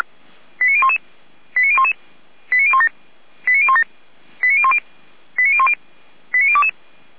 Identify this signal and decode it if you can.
Five-tone selective call (ZVEI), heard as quick runs of five tones, tones of about 70 ms. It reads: 89011 89011 89017 89017 89010 89010 89022